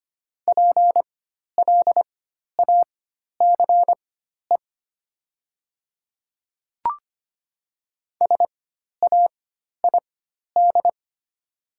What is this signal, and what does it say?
Morse code, heard as a keyed tone, about 25 wpm, spaced as for 12 wpm, 700 Hz. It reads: PLACE SAID
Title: 4x Character Spacing - Top 200 Words - 25wpm